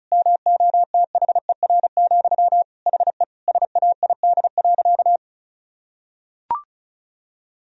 Morse code, heard as a keyed tone, about 36 wpm, 700 Hz. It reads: MOTHER, HE SAID.
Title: Sentences from Top 1000 Words - Review Part 4 - 35wpm